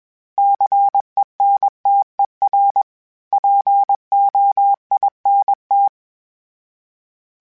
Morse code, heard as a keyed tone, about 21 wpm, 800 Hz. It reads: CENTER POINT
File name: fromSMs / FromSMs mp3s / Cities@20